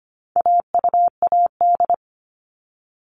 Morse code, heard as a keyed tone, about 25 wpm, 700 Hz. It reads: AUAD
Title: All Cut Numbers - 4-Digit 25wpm